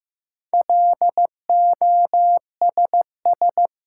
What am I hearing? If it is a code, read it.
Morse code, heard as a keyed tone, about 15 wpm, 700 Hz. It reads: LOSS
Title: Sentences from Top 1400 Words - New Word Drill Alt - 15wpm